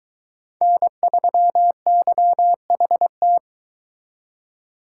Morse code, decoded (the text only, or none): N3YHT